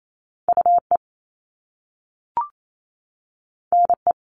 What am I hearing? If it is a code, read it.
Morse code, heard as a keyed tone, about 28 wpm, 700 Hz. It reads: U E N E
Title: Top Two Letter Combinations in Words 28wpm